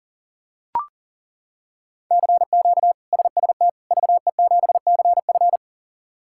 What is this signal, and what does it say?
Morse code, heard as a keyed tone, about 40 wpm, 700 Hz. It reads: CQ SST VE7CF
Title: K1USN SST QSO Practice - 40wpm